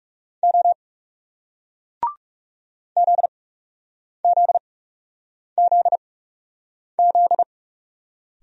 Morse code, heard as a keyed tone, about 41 wpm, spaced as for 8 wpm, 700 Hz. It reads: O ZZZZ